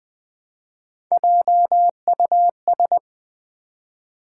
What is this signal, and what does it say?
Morse code, heard as a keyed tone, about 20 wpm, 700 Hz. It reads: JUS
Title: Build A Word - 1 Letter At A Time - 4-Letter Words - 20wpm